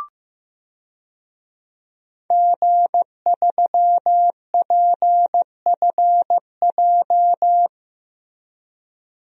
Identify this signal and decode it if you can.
Morse code, heard as a keyed tone, about 15 wpm, 700 Hz. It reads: G3PFJ
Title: DXCC-100 Call Signs - 5 Characters - Repeated 2x - 15wpm